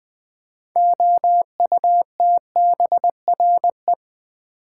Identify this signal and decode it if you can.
Morse code, heard as a keyed tone, about 21 wpm, 700 Hz. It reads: OUTBRE
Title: Build A Word - 2 Letters At A Time - 8-Letter Words - 20wpm